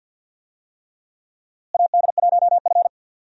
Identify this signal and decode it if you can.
Morse code, heard as a keyed tone, about 51 wpm, 700 Hz. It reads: AD1F
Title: USA Call Signs - 4 Characters - Repeated 3x - 50wpm